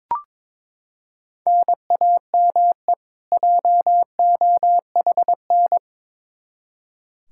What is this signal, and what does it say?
Morse code, heard as a keyed tone, about 22 wpm, 700 Hz. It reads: NAME JOHN
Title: Practice QSO - Exchange 1 - Beginning - 22wpm